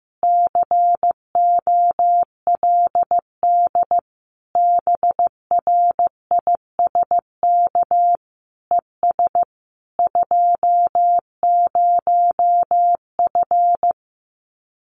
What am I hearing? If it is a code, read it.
Morse code, heard as a keyed tone, about 15 wpm, 700 Hz. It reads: COLD BRISK ES 20F
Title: Common QSO Phrases Encore 15wpm